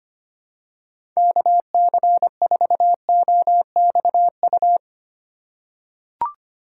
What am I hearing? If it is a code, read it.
Morse code, heard as a keyed tone, about 25 wpm, 700 Hz. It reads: KC4OXU
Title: USA Call Signs - 6 Characters - 3x Character Spacing - 25wpm